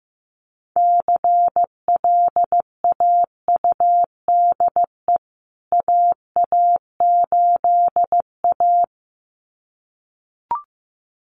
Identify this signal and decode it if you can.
Morse code, heard as a keyed tone, about 14 wpm, 700 Hz. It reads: CLAUDE AA8A
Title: CWT Exchange 15wpm